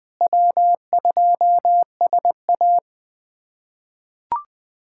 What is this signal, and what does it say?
Morse code, heard as a keyed tone, about 21 wpm, 700 Hz. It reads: W2SA
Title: K1USN SST QSO Practice - 20wpm